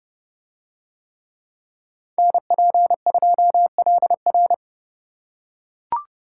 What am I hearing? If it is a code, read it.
Morse code, heard as a keyed tone, about 28 wpm, 700 Hz. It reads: NP2LR